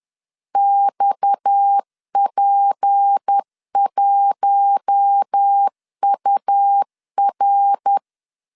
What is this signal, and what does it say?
Morse code, heard as a keyed tone, about 10 wpm, 800 Hz. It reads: XP1UR